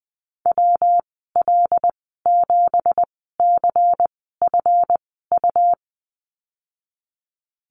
Morse code, heard as a keyed tone, about 20 wpm, spaced as for 14 wpm, 700 Hz. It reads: WL7CFU
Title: 245 Lesson 39g - Letter z - Call Signs 20wpm